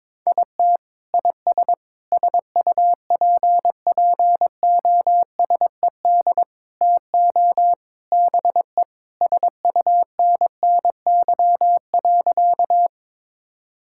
Morse code, heard as a keyed tone, about 22 wpm, 700 Hz. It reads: IT IS SUPPOSED TO BE SUNNY.
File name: practice / 22wpm / Sentences with 3-Word Phrases - 22wpm